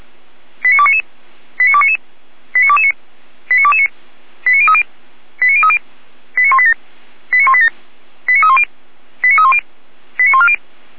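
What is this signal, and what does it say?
Five-tone selective call (ZVEI), heard as quick runs of five tones, tones of about 70 ms. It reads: 89299 89299 89229 89229 89030 89030 89187 89187 89310 89310 89140
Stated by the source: ZVEI five-tone selective call 89337, received by SDR.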